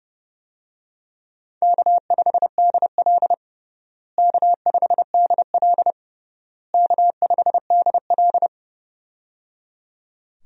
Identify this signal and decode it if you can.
Morse code, heard as a keyed tone, about 30 wpm, 700 Hz. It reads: K5DL K5DL K5DL
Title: USA Call Signs - 4 Characters - Repeated 3x - 30wpm